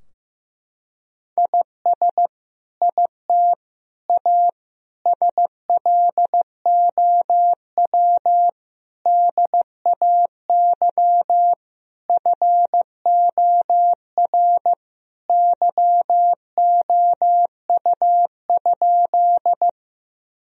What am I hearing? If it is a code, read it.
Morse code, heard as a keyed tone, about 15 wpm, 700 Hz. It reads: IS IT A SLOW DAY FOR YOU?
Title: Sentences from Top 1000 Words - Review Part 2 - 15wpm